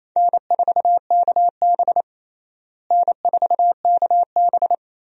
Morse code, heard as a keyed tone, about 27 wpm, 700 Hz. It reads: N4KB N4KB
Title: USA Call Signs - 4 Characters - Repeated 2x - 28wpm